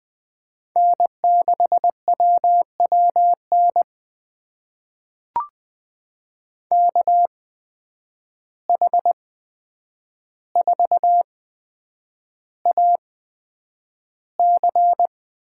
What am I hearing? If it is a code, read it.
Morse code, heard as a keyed tone, about 20 wpm, spaced as for 9 wpm, 700 Hz. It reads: N6WWN KH4AC
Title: USA Call Signs - 5 Characters - 8x Character Spacing - 20wpm